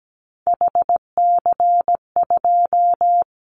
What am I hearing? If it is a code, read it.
Morse code, heard as a keyed tone, about 17 wpm, 700 Hz. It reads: HC2
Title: Callsign Prefixes for Top 100 DXCC - 17wpm